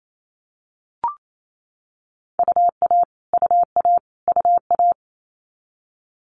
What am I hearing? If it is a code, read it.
Morse code, heard as a keyed tone, about 28 wpm, 700 Hz. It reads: UA UA UA
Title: Multi-letter Phonemes Repeated 3x 28wpm